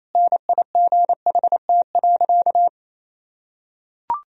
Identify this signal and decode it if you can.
Morse code, heard as a keyed tone, about 28 wpm, 700 Hz. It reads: NIGHT.
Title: Sentences from Top 1000 Words - Review Part 2 - 28wpm